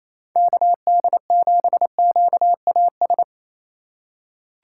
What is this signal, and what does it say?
Morse code, heard as a keyed tone, about 28 wpm, 690 Hz. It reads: KD7QAS
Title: USA Call Signs - 6 Characters - 2x Character Spacing - 28wpm